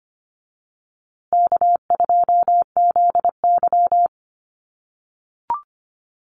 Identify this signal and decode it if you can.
Morse code, heard as a keyed tone, about 25 wpm, 700 Hz. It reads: K2ZY
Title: USA Call Signs Repeated 2x 25wpm